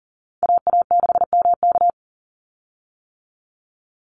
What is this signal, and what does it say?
Morse code, heard as a keyed tone, about 40 wpm, 700 Hz. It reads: AA6MK
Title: Call Signs - 40wpm